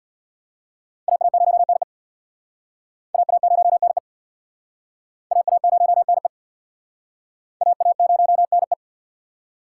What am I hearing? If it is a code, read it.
Morse code, heard as a keyed tone, about 63 wpm, 700 Hz. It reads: AA0NE AA0NE AA0NE AA0NE